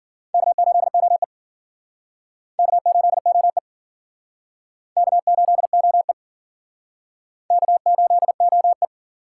Morse code, heard as a keyed tone, about 54 wpm, 700 Hz. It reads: K8OE K8OE K8OE K8OE